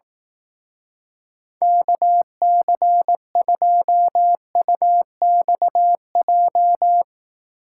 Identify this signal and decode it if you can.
Morse code, heard as a keyed tone, about 18 wpm, 700 Hz. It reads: KC2UXJ
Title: Speed Racing - USA Call Signs 15wpm